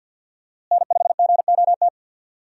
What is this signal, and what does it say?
Morse code, heard as a keyed tone, about 52 wpm, 700 Hz. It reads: N5GOT
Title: USA Call Signs - 5 Characters - Repeated 2x - 50wpm